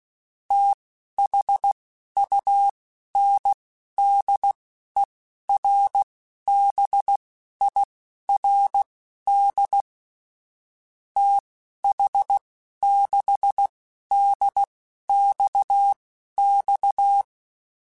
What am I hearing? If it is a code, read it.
Morse code, heard as a keyed tone, about 16 wpm, spaced as for 11 wpm, 800 Hz. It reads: THUNDERBIRD TH6DXX
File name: (spaced) / QSOs Spaced-mp3s / WD0BGP 1 @15 sp